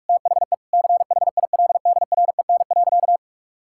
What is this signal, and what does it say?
Morse code, heard as a keyed tone, about 46 wpm, 700 Hz. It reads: THE CHILDREN.
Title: Sentences from Top 1000 Words - Review Part 5 - 45wpm